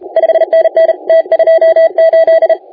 Morse code, heard as a keyed tone, about 32 wpm, 650 Hz. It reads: 5NN T28
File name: la4c / sac-cw-2018 / CONTEST AUDIO 1600Z